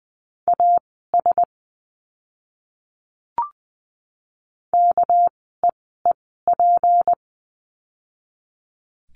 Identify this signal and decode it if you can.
Morse code, heard as a keyed tone, about 20 wpm, spaced as for 14 wpm, 700 Hz. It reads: AS KEEP
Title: 199 Lesson 32f - Letter k - Words 20wpm